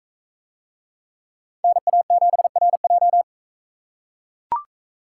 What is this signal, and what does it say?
Morse code, heard as a keyed tone, about 41 wpm, 700 Hz. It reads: NA7RJ